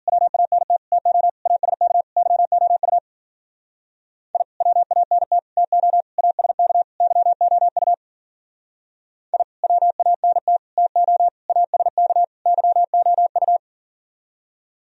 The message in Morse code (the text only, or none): WANT TO ASK YOU I WANT TO ASK YOU I WANT TO ASK YOU